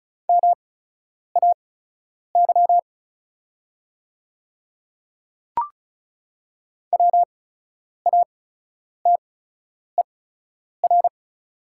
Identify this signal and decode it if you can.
Morse code, heard as a keyed tone, about 36 wpm, spaced as for 10 wpm, 700 Hz. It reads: MAY WATER
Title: 8x Character Spacing - Top 100 Words - 35wpm